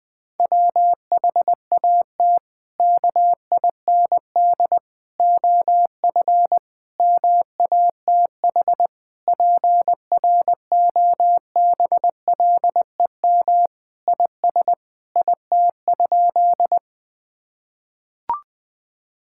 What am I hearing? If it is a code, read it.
Morse code, heard as a keyed tone, about 21 wpm, 700 Hz. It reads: WHAT KIND OF MATH PROBLEM IS IT?
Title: Sentences from Top 1000 Words - Review Part 6 - 20wpm